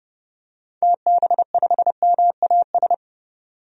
Morse code, heard as a keyed tone, about 30 wpm, 700 Hz. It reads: TB5MAS
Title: DXCC-100 Call Signs - 6 Characters - 30wpm